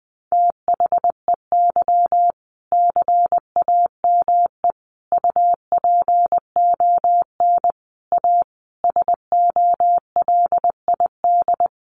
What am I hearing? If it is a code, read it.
Morse code, heard as a keyed tone, about 20 wpm, 700 Hz. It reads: THEY CAME UPON A SOLID
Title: Sentences from Top 1500 Words 20wpm